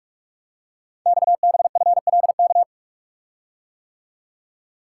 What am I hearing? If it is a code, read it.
Morse code, heard as a keyed tone, about 44 wpm, 690 Hz. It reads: XBFLK
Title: Code Groups - 5 Letters - 45wpm